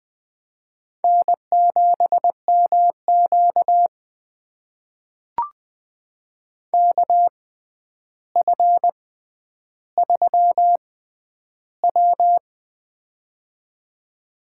Morse code, heard as a keyed tone, about 20 wpm, spaced as for 10 wpm, 700 Hz. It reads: N7MQ KF3W
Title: USA Call Signs - 4 Characters - 6x Character Spacing - 20wpm